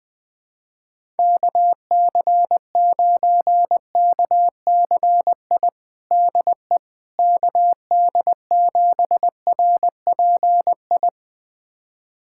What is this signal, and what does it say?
Morse code, heard as a keyed tone, about 21 wpm, 700 Hz. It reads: KC9KCI DE KD7RPI
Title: USA Callsign DE Callsign 20wpm